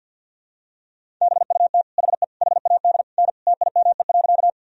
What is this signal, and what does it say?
Morse code, heard as a keyed tone, about 52 wpm, 700 Hz. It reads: BUT HE HAD N TIME.
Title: Sentences from Top 100 Words and Abbreviations 50wpm